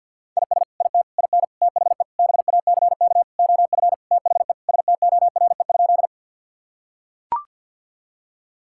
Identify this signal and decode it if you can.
Morse code, heard as a keyed tone, about 45 wpm, 700 Hz. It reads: IS IT IN THE BACK OF THE STORE?
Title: Sentences from Top 1000 Words - Review Part 5 - 50wpm